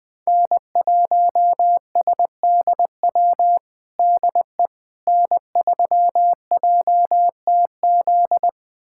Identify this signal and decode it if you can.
Morse code, heard as a keyed tone, about 20 wpm, 700 Hz. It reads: N1SDW DE N3JTZ
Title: USA Callsign DE Callsign - 5 Characters - 20wpm